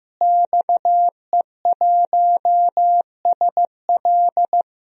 Morse code, heard as a keyed tone, about 15 wpm, 700 Hz. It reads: XE1SL
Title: DXCC-100 Call Signs - 5 Characters - Repeated 3x - 15wpm